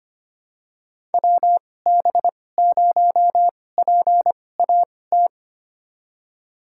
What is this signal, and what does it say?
Morse code, heard as a keyed tone, about 25 wpm, spaced as for 18 wpm, 700 Hz. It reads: WB0PAT
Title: USA Call Signs - 6 Characters - 2x Character Spacing - 25wpm